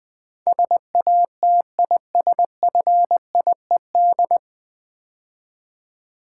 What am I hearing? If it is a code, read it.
Morse code, heard as a keyed tone, about 20 wpm, 700 Hz. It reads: SATISFIED
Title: Build A Word - 3 Letters At A Time - 9-Letter Words - 20wpm